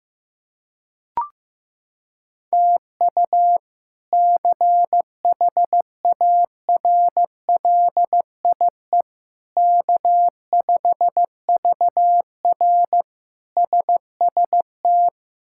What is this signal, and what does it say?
Morse code, heard as a keyed tone, about 15 wpm, 700 Hz. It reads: TU CHARLIE K5VR SST
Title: K1USN SST QSO Practice - 15wpm